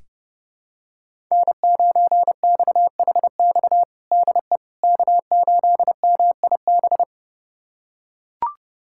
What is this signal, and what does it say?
Morse code, heard as a keyed tone, about 32 wpm, 700 Hz. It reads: N9XHX DE K8MIB
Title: USA Callsign DE Callsign - 5 Characters - 30wpm